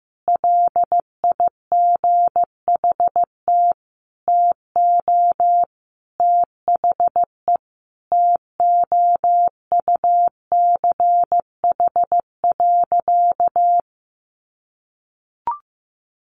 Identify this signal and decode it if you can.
Morse code, heard as a keyed tone, about 15 wpm, 700 Hz. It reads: LIGHT TO THE TOUCH.
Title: Sentences from Top 1500 Words 15wpm